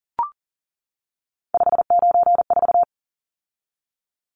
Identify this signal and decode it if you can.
Morse code, heard as a keyed tone, about 40 wpm, 700 Hz. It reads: 594
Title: Numbers - 3-Digit 40wpm